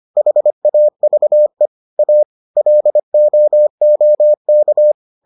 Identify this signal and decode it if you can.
Morse code, heard as a keyed tone, about 25 wpm, 600 Hz. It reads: HAVE A LOOK